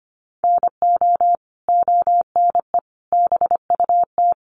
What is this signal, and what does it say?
Morse code, heard as a keyed tone, about 25 wpm, 700 Hz. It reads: NO ONE BUT
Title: Sentences from Top 1300 Words 25wpm